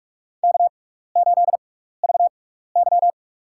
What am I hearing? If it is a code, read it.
Morse code, heard as a keyed tone, about 43 wpm, spaced as for 15 wpm, 700 Hz. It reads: K8VY
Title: USA Call Signs - 4 Characters - 6x Character Spacing - 45wpm